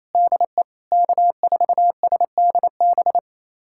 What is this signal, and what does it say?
Morse code, heard as a keyed tone, about 28 wpm, 700 Hz. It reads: DE K4SDB
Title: USA Callsign DE Callsign - 5 Characters - 28wpm